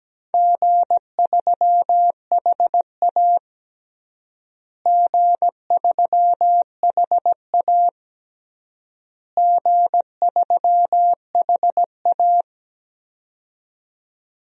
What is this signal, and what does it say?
Morse code, heard as a keyed tone, about 17 wpm, 700 Hz. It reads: G3HA G3HA G3HA
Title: DXCC-100 Call Signs - 4 Characters - Repeated 3x - 17wpm